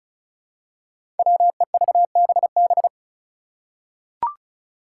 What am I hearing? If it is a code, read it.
Morse code, heard as a keyed tone, about 36 wpm, 700 Hz. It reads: WEVBB
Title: Five Challenging Letter-Number Combinations 35wpm